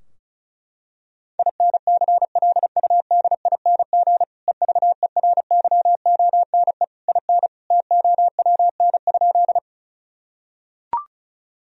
Morse code, heard as a keyed tone, about 36 wpm, 700 Hz. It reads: INCLUDING EVERYONE IN TOWN?
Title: Sentences from Top 1000 Words - Review Part 4 - 35wpm